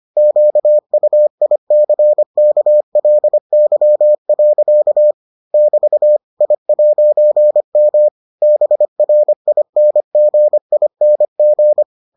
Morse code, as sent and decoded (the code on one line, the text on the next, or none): --.- ..- .. -.-. -.- .-.. -.-- .-.-.- / -...- / .. .----. -- / -... .-. .. -. --. .. -. --.
QUICKLY. <BT> I'M BRINGING